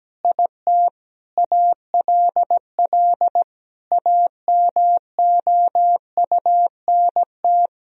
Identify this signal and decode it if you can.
Morse code, heard as a keyed tone, about 17 wpm, 700 Hz. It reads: IT ALL AMOUNT
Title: Sentences from Top 1000 Words - Review Part 4 - 17wpm